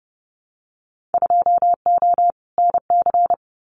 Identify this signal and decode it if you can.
Morse code, heard as a keyed tone, about 30 wpm, 700 Hz. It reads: 2O NC
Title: Winter Field Day Exchanges 30wpm